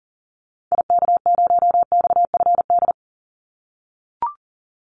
Morse code, heard as a keyed tone, about 40 wpm, 700 Hz. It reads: IK0XFD
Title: Call Signs - 40wpm